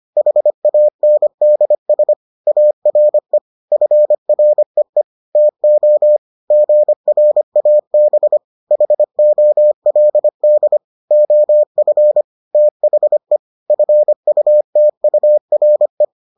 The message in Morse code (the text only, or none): HANDS ARE FREE TO GRAB HOLD OF THE FUTURE